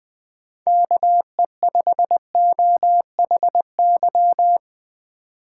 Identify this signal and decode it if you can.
Morse code, heard as a keyed tone, about 20 wpm, 700 Hz. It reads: KE5OHY
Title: USA Call Signs - 6 Characters - 4x Character Spacing - 20wpm